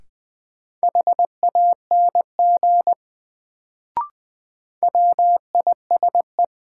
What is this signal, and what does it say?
Morse code, heard as a keyed tone, about 20 wpm, 700 Hz. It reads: HANG WISE
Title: Sentences from Top 1600 Words - New Word Drill - 20wpm